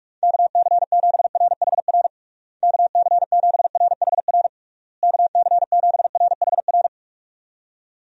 Morse code, heard as a keyed tone, about 41 wpm, 700 Hz. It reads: KC7RHR KC7RHR KC7RHR
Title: USA Call Signs - 6 Characters - Repeated 3x - 45wpm